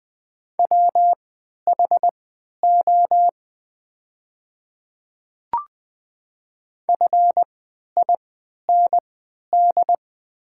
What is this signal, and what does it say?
Morse code, heard as a keyed tone, about 20 wpm, spaced as for 11 wpm, 700 Hz. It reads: WHO FIND
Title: 3x Character Spacing - Top 100 Words - 20wpm